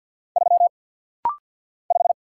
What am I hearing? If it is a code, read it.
Morse code, heard as a keyed tone, about 50 wpm, 700 Hz. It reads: 3 5
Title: Single Letter-Number - Rapid-Fire 50wpm